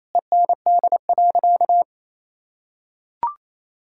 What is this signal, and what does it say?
Morse code, heard as a keyed tone, about 28 wpm, 700 Hz. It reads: END.
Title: Sentences from Top 1600 Words 28wpm